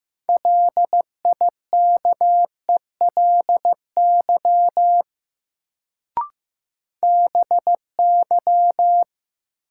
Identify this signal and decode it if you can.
Morse code, heard as a keyed tone, about 15 wpm, 700 Hz. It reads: LIKELY BY
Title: Sets of 2 Words - New Word Drill - 15wpm